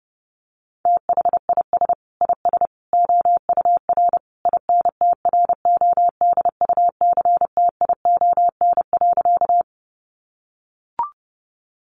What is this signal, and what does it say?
Morse code, heard as a keyed tone, about 28 wpm, 700 Hz. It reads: THIS IS OUR INTRODUCTION.
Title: Sentences from Top 1000 Words 30wpm